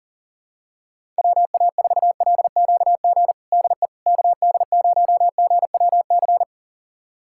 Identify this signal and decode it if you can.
Morse code, heard as a keyed tone, about 40 wpm, 700 Hz. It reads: WA4LQG DE KD0GWC
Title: USA Callsign DE Callsign - 6 Characters - 40wpm